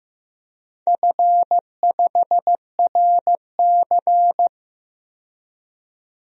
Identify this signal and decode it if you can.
Morse code, heard as a keyed tone, about 15 wpm, 700 Hz. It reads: F5RC
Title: DXCC-100 Call Signs - 4 Characters - 15wpm